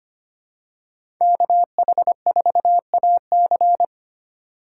Morse code, heard as a keyed tone, about 25 wpm, 700 Hz. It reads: KH4AC
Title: USA Call Signs - 5 Characters - 4x Character Spacing - 25wpm